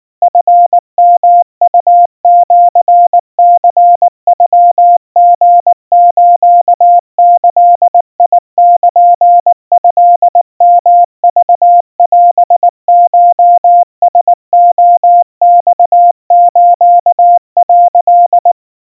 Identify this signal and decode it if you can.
Morse code (Japanese Wabun code), heard as a keyed tone, about 19 wpm, 700 Hz. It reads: チヨウシニノリスギルトヨクオコラレマス」
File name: kawa / Morse sample / kamyu 70 cpm 700 Hz Crispy